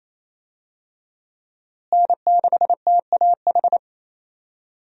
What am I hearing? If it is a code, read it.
Morse code, heard as a keyed tone, about 28 wpm, 700 Hz. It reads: N6TAH
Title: USA Call Signs - 5 Characters - Repeated 2x - 28wpm